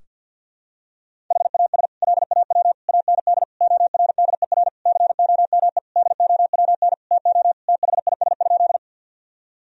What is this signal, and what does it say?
Morse code, heard as a keyed tone, about 52 wpm, 700 Hz. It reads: HAS LAW AND ORDER COME DOWN TO THIS?
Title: Sentences from Top 1000 Words - Review Part 3 - 50wpm